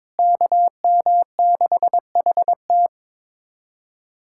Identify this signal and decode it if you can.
Morse code, heard as a keyed tone, about 22 wpm, 700 Hz. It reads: KM6HT
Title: USA Callsign DE Callsign 22wpm